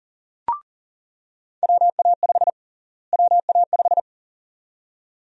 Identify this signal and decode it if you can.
Morse code, heard as a keyed tone, about 40 wpm, 700 Hz. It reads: WA5 WA5
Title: USA Call Sign Prefixes - Repeated 2x - 40wpm